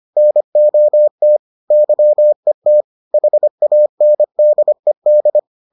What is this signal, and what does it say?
Morse code, heard as a keyed tone, about 25 wpm, 600 Hz. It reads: NOT YET HANDED